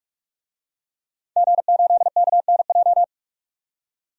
Morse code, heard as a keyed tone, about 45 wpm, 700 Hz. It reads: G8KNJ
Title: DXCC-100 Call Signs - 5 Characters - 45wpm